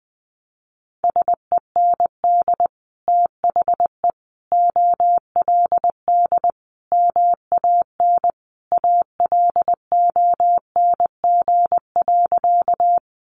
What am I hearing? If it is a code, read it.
Morse code, heard as a keyed tone, about 20 wpm, 700 Hz. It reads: SEND THE OLD MAN ALONG.
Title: Sentences from Top 1500 Words 20wpm